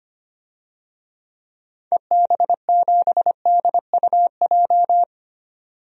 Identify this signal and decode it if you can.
Morse code, heard as a keyed tone, about 25 wpm, 690 Hz. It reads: EB7DUJ
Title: DXCC-100 Call Signs - 6 Characters - 25wpm